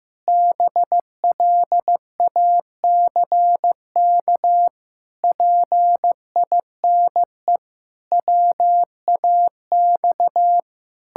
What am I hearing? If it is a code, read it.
Morse code, heard as a keyed tone, about 16 wpm, 700 Hz. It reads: BLACK PINE WAX
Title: Pangram Challenge v2 - 15wpm